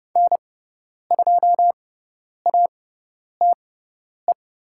Morse code, heard as a keyed tone, about 30 wpm, spaced as for 10 wpm, 700 Hz. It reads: N2ATE
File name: Lesson 16g - Letter c - Call Signs 30wpm